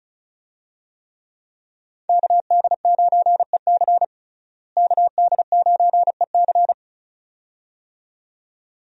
Morse code, heard as a keyed tone, about 34 wpm, 700 Hz. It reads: KD9EC KD9EC
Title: USA Call Signs - 5 Characters - Repeated 2x - 35wpm